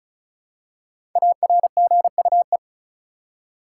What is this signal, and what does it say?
Morse code, heard as a keyed tone, about 34 wpm, 700 Hz. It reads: ARGUE
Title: Five Letter Words - 35wpm